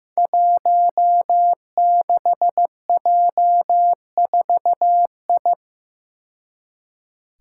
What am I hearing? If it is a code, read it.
Morse code, heard as a keyed tone, about 15 wpm, 700 Hz. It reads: 16J4I
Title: Five Challenging Letter-Number Combinations 15wpm